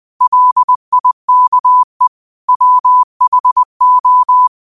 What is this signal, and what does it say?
Morse code, heard as a keyed tone, about 20 wpm, 1000 Hz. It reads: LIKE WHO
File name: qotdfiles / 20wpm